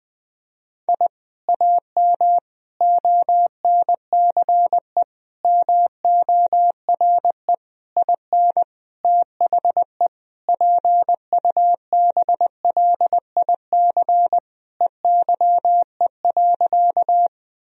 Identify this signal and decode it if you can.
Morse code, heard as a keyed tone, about 20 wpm, 700 Hz. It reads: I AM ONCE MORE IN THE PUBLIC EYE.
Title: Sentences from Top 1000 Words - Review Part 3 - 20wpm